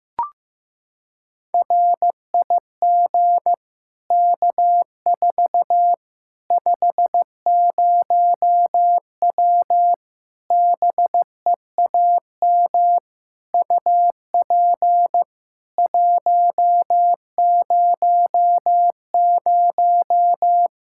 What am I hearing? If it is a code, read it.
Morse code, heard as a keyed tone, about 15 wpm, 700 Hz. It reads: RIG K4 50W BEAM UP 100
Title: Practice QSO - Exchange 2 - Info - 15wpm